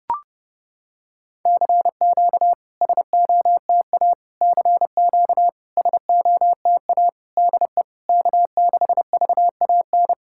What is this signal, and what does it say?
Morse code, heard as a keyed tone, about 30 wpm, 700 Hz. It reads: CQ SOTA CQ SOTA DE K6VAN
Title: SOTA QSO Practice 30wpm